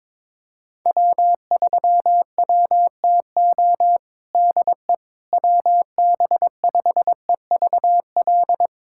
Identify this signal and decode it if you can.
Morse code, heard as a keyed tone, about 22 wpm, 700 Hz. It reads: W3WTO DE WB5EVL
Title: USA Callsign DE Callsign 22wpm